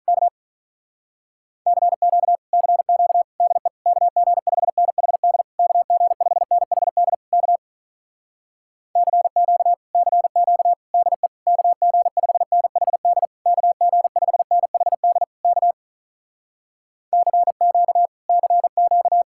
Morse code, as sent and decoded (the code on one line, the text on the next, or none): -.- / -.-. --.- / -.-. --.- / -.. / -.- --. ..... -. .... -.. / -.- --. ..... -. .... -.. / -.- / -.-. --.- / -.-. --.- / -.. / -.- --. ..... -. .... -.. / -.- --. ..... -. .... -.. / -.- / -.-. --.- / -.-. --.-
K CQ CQ DE KG5NHD KG5NHD K CQ CQ DE KG5NHD KG5NHD K CQ CQ